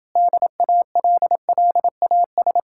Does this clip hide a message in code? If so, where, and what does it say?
Yes: Morse code from 0.2 to 2.6 s, heard as a keyed tone, about 30 wpm, 700 Hz. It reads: DALLAS